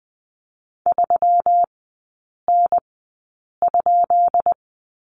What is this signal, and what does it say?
Morse code, heard as a keyed tone, about 20 wpm, spaced as for 8 wpm, 700 Hz. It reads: 3N?
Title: 198 Lesson 32e - Letter k - Three Characters 20wpm